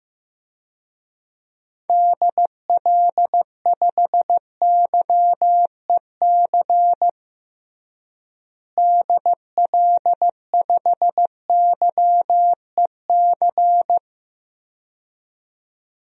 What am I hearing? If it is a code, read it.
Morse code, heard as a keyed tone, about 15 wpm, 700 Hz. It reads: DL5YEC DL5YEC
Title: DXCC-100 Call Signs - 6 Characters - Repeated 2x - 15wpm